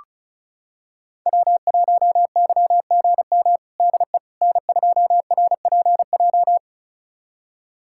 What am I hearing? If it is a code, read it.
Morse code, heard as a keyed tone, about 36 wpm, 700 Hz. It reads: W1YGM DE N2RPJ
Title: USA Callsign DE Callsign - 5 Characters - 35wpm